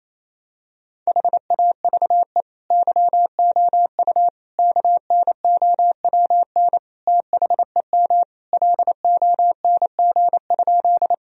Morse code, heard as a keyed tone, about 27 wpm, 700 Hz. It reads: HAVE YOU KNOWN THEM LONG?
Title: Sentences from Top 1000 Words - Review Part 3 - 28wpm